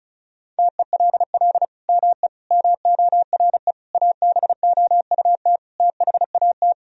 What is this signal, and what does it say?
Morse code, heard as a keyed tone, about 36 wpm, 700 Hz. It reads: TELL ME MORE ABOUT THAT
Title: Sentences from Top 1000 Words - Review Part 4 - 35wpm